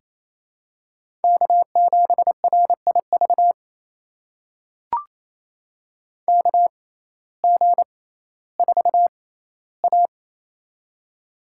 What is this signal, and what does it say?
Morse code, heard as a keyed tone, about 27 wpm, spaced as for 15 wpm, 700 Hz. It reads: K7RIV KG4A
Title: USA Call Signs - 6x Character Spacing - 28wpm